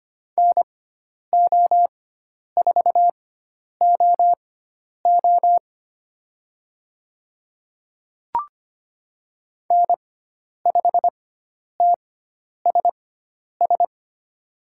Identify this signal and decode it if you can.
Morse code, heard as a keyed tone, about 25 wpm, spaced as for 10 wpm, 700 Hz. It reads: NO4OO N5TSS
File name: Lesson 16g - Letter c - Call Signs 25wpm